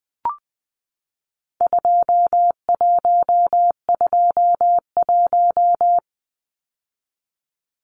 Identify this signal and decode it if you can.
Morse code, heard as a keyed tone, about 20 wpm, 700 Hz. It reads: 2121
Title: Numbers - 4-Digit 20wpm